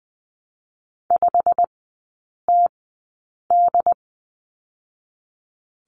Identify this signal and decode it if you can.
Morse code, heard as a keyed tone, about 20 wpm, spaced as for 8 wpm, 690 Hz. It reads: 5TD
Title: 087 Lesson 15e - Number 5 - Three Characters 20wpm